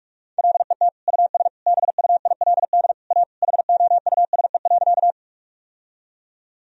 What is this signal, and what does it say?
Morse code, heard as a keyed tone, about 46 wpm, 700 Hz. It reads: LET US BUILD A HOUSE.
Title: Sentences from Top 1000 Words - Review Part 1 - 45wpm